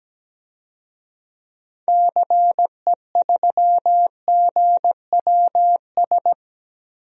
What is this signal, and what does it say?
Morse code, heard as a keyed tone, about 17 wpm, 700 Hz. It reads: CE3GWS
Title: DXCC-100 Call Signs - 6 Characters - 17wpm